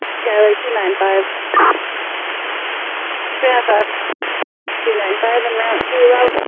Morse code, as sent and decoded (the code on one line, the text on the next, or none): none